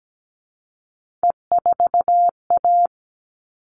Morse code, heard as a keyed tone, about 17 wpm, 700 Hz. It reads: E4A
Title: All Cut Numbers - 3-Digit 17wpm